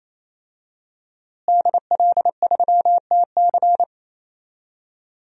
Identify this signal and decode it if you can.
Morse code, heard as a keyed tone, about 27 wpm, 700 Hz. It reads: DL3TC
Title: DXCC-100 Call Signs - 5 Characters - Repeated 3x - 28wpm